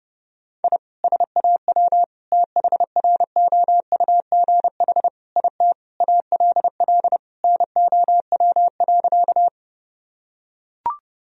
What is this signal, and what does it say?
Morse code, heard as a keyed tone, about 30 wpm, 700 Hz. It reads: I SAW THROUGH IT ALL NOW.
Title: Sentences from Top 1000 Words - Review Part 1 - 30wpm